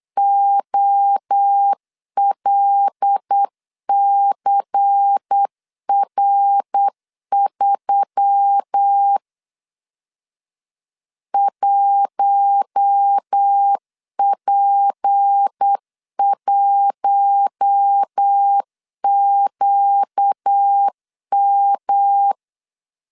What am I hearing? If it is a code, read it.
Morse code, heard as a keyed tone, about 8 wpm, 800 Hz. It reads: OLCR3 1P1QM
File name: LEZIONI / 6 PKX0 / 6 P 8wpm 3025